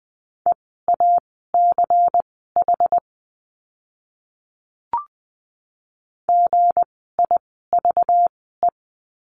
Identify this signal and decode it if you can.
Morse code, heard as a keyed tone, about 20 wpm, spaced as for 14 wpm, 700 Hz. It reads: EACH GIVE
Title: 192 Lesson 31f - Letter v - Words 20wpm